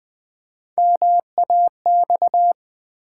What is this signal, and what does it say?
Morse code, heard as a keyed tone, about 20 wpm, 700 Hz. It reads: MAX
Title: Build A Word - 3 Letters At A Time - 8-Letter Words - 20wpm